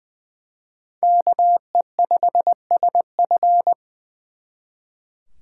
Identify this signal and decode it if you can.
Morse code, heard as a keyed tone, about 24 wpm, 700 Hz. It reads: KE5SF
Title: Speed Racing - USA 2x2 Call Signs 20wpm